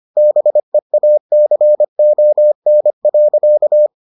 Morse code, heard as a keyed tone, about 25 wpm, 600 Hz. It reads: BEACON.